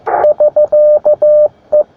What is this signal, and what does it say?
Morse code, heard as a keyed tone, about 15 wpm, 600 Hz. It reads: <SK>E
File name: attachments / sp4ipl 20m